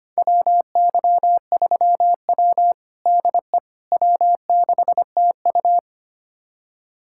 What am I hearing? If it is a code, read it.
Morse code, heard as a keyed tone, about 25 wpm, 700 Hz. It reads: WY3W DE W6TU
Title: USA Callsign DE Callsign - 4 Characters - 25wpm